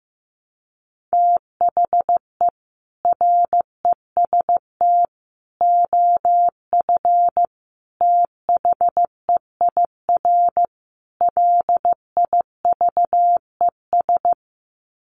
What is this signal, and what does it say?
Morse code, heard as a keyed tone, about 15 wpm, 700 Hz. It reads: THE REST OF THEIR LIVES
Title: Sets of 5 Words Encore 15wpm